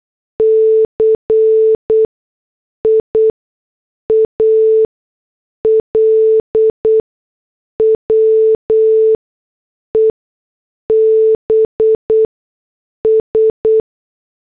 Morse code, heard as a keyed tone, about 8 wpm, spaced as for 6 wpm, 440 Hz. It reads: CIAL WEBS